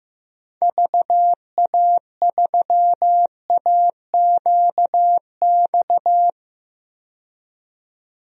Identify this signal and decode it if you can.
Morse code, heard as a keyed tone, about 15 wpm, 700 Hz. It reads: VA3AQX
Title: DXCC-100 Call Signs - 6 Characters - 15wpm